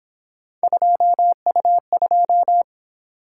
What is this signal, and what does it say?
Morse code, heard as a keyed tone, about 26 wpm, 700 Hz. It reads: 2U2
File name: Speed-Racing - Three Letter-Number Combinations - 22wpm